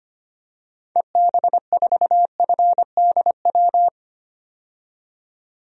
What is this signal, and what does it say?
Morse code, heard as a keyed tone, about 25 wpm, 700 Hz. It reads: EB4FDW
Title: DXCC-100 Call Signs - 6 Characters - 25wpm